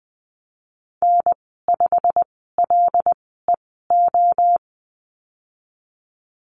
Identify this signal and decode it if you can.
Morse code, heard as a keyed tone, about 20 wpm, spaced as for 15 wpm, 700 Hz. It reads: N5LEO
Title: 088 Lesson 15g - Number 5 - Call Signs 20wpm